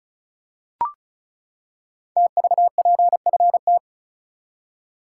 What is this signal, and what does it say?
Morse code, heard as a keyed tone, about 36 wpm, 700 Hz. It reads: TVPFT
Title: Code Groups - 5 Letters - 35wpm